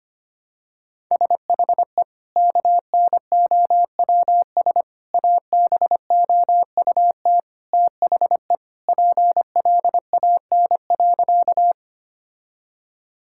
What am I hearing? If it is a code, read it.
Morse code, heard as a keyed tone, about 25 wpm, 700 Hz. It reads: SHE KNOWS ABOUT THE PLAN.
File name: practice / 25wpm / Sentences with 2-Word Phrases - 25wpm